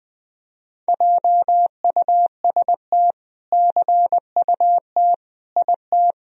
Morse code, heard as a keyed tone, about 20 wpm, 700 Hz. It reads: JUST CUT IT
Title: Sentences from Top 300 Words and Abbreviations 20wpm